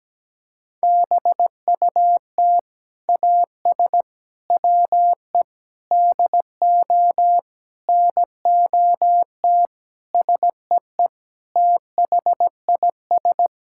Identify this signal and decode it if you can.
Morse code, heard as a keyed tone, about 17 wpm, 700 Hz. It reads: BUT AS WE DO NOT SEE THIS
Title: Sentences from Top 1600 Words 17wpm